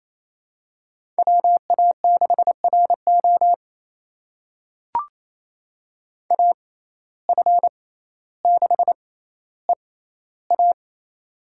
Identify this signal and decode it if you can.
Morse code, heard as a keyed tone, about 28 wpm, spaced as for 14 wpm, 700 Hz. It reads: WA6RO AF6EA
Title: USA Call Signs - 5 Characters - 6x Character Spacing - 28wpm